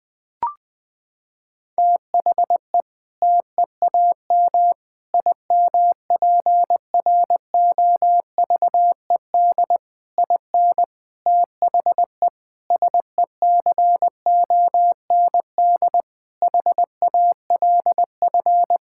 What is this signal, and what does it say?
Morse code, heard as a keyed tone, about 20 wpm, 700 Hz. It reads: THE TEAM IMPROVED IN THE SECOND HALF
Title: Sentences with 7-Word Phrases - 20wpm